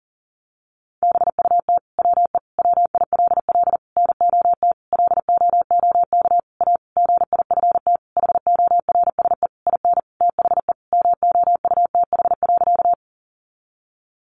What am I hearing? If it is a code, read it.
Morse code, heard as a keyed tone, about 40 wpm, 700 Hz. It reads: BUT WE WILL NOT LOOK A GIFT HORSE IN THE MOUTH.
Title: Sentences from Top 1500 Words 40wpm